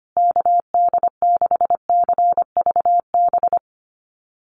Morse code, heard as a keyed tone, about 25 wpm, 700 Hz. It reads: KD6CVB
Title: USA Call Signs Repeated 3x 25wpm